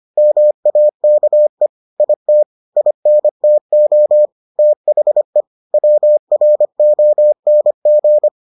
Morse code, heard as a keyed tone, about 25 wpm, 600 Hz. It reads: MAKE IT INTO THE WRONG